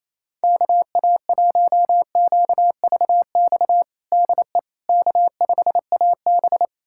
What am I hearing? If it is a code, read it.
Morse code, heard as a keyed tone, about 28 wpm, 700 Hz. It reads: KA1QVX DE K5AB
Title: USA Callsign DE Callsign 28wpm